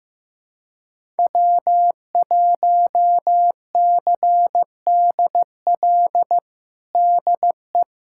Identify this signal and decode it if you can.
Morse code, heard as a keyed tone, about 15 wpm, 700 Hz. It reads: W1CDL DE
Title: USA Callsign DE Callsign - 5 Characters - 15wpm